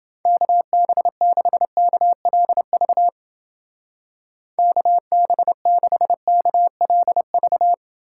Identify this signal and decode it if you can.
Morse code, heard as a keyed tone, about 28 wpm, 700 Hz. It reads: KB6KLV KB6KLV